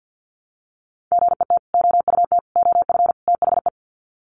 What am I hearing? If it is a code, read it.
Morse code, heard as a keyed tone, about 48 wpm, 700 Hz. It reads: GET OUT OF THE
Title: Sets of 4 Words 50wpm